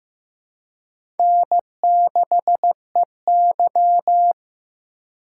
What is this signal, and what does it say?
Morse code, heard as a keyed tone, about 15 wpm, 700 Hz. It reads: N6EY
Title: USA Call Signs - 4 Characters - 3x Character Spacing - 15wpm